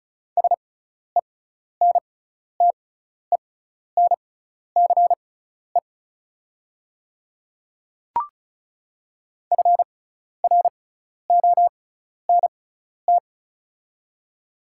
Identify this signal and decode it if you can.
Morse code, heard as a keyed tone, about 34 wpm, spaced as for 12 wpm, 690 Hz. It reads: SENTENCE FRONT
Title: 6x Character Spacing - Top 500 Words - 35wpm